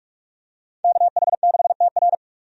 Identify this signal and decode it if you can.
Morse code, heard as a keyed tone, about 48 wpm, 700 Hz. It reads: KH6TR
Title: DXCC-100 Call Signs - 5 Characters - 45wpm